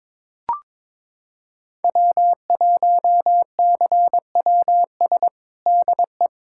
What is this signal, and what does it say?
Morse code, heard as a keyed tone, about 21 wpm, 700 Hz. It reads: W1CWS DE
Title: USA Callsign DE Callsign - 5 Characters - 22wpm